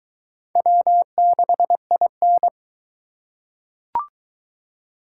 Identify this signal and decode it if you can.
Morse code, heard as a keyed tone, about 22 wpm, 700 Hz. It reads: W6IN